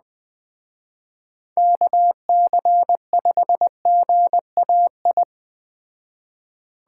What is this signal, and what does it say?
Morse code, heard as a keyed tone, about 20 wpm, 700 Hz. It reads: KC5GAI
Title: Speed Racing - USA 2x3 Call Signs 17wpm